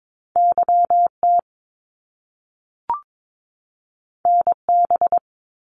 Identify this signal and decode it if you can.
Morse code, heard as a keyed tone, about 22 wpm, 700 Hz. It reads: YT NB
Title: Canadian Provinces and Territories 22wpm